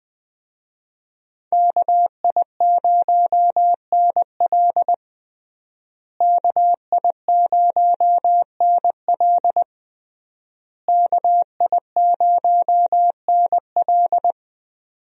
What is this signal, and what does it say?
Morse code, heard as a keyed tone, about 20 wpm, 700 Hz. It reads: KI0NL KI0NL KI0NL
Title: USA Call Signs - 5 Characters - Repeated 3x - 20wpm